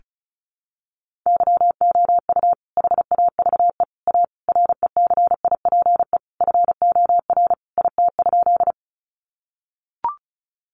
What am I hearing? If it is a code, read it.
Morse code, heard as a keyed tone, about 34 wpm, 700 Hz. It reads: YOU HAVE A RECIPE FOR IT?
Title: Sentences from Top 1200 Words 35wpm